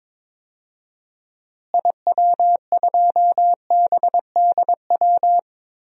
Morse code, heard as a keyed tone, about 22 wpm, 700 Hz. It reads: IW2BDW
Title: DXCC-100 Call Signs - 6 Characters - Repeated 3x - 22wpm